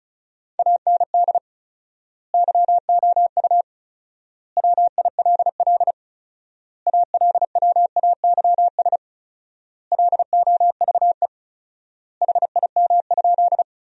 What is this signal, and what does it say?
Morse code, heard as a keyed tone, about 36 wpm, 700 Hz. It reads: AND YOU WILL ALWAYS LOVE HIM?
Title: 4x Word-Spacing - Sentences from Top 400 Words - 35wpm